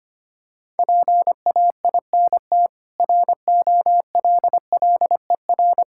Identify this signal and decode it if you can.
Morse code, heard as a keyed tone, about 25 wpm, 700 Hz. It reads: PAINT ROLLER
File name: practice / 25wpm / Difficult Words with Hints - 25wpm